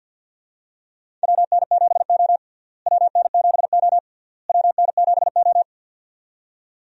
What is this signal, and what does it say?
Morse code, heard as a keyed tone, about 52 wpm, 700 Hz. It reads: WN7O WN7O WN7O
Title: USA Call Signs - 4 Characters - Repeated 3x - 50wpm